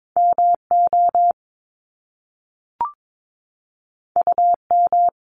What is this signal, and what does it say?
Morse code, heard as a keyed tone, about 22 wpm, 700 Hz. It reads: MO UM
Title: Top Two Letter Combinations in Words 22wpm